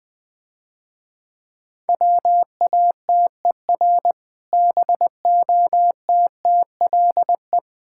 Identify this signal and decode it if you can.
Morse code, heard as a keyed tone, about 22 wpm, 700 Hz. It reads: WATER BOTTLE
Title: Easy Words - Speed-Racing with Hints - 20wpm